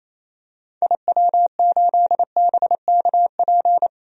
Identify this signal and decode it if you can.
Morse code, heard as a keyed tone, about 28 wpm, 700 Hz. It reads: IW8BKP
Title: DXCC-100 Call Signs - 6 Characters - 28wpm